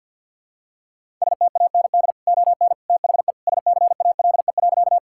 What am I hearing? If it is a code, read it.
Morse code, heard as a keyed tone, about 51 wpm, 700 Hz. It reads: STAND ON THE SCALE.
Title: Sentences from Top 1000 Words - Review Part 5 - 50wpm